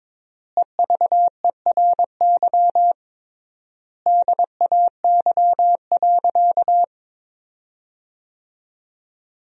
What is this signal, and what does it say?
Morse code, heard as a keyed tone, about 21 wpm, 700 Hz. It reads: EVERY DAY.
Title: 3x Word-Spacing - Sentences from Top 200 Words - 22wpm